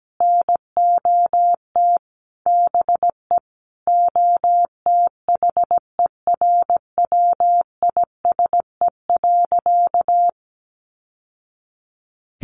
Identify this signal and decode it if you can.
Morse code, heard as a keyed tone, about 17 wpm, 700 Hz. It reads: NOT BE OTHERWISE.
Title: Sentences from Top 1200 Words 17wpm